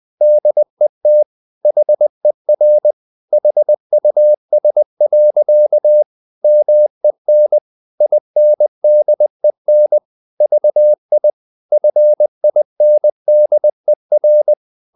Morse code, heard as a keyed tone, about 20 wpm, 600 Hz. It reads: DET HER HUS. MEN INDEN VI FINDER
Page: Ferie 20 wpm